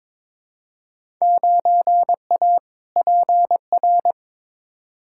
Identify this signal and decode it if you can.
Morse code, heard as a keyed tone, about 21 wpm, 700 Hz. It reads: 9A PR
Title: Field Day Exchanges Encore - 22wpm